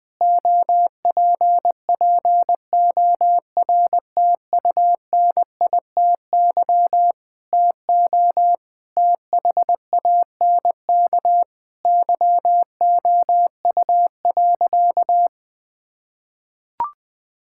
Morse code, heard as a keyed tone, about 19 wpm, 700 Hz. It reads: OPPORTUNITY TO THANK YOU.
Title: Sentences with 8-Word Phrases - 20wpm